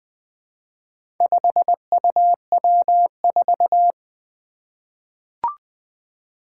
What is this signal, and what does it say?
Morse code, heard as a keyed tone, about 20 wpm, 700 Hz. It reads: 5UW4
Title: Four Challenging Letter-Number Combinations 20wpm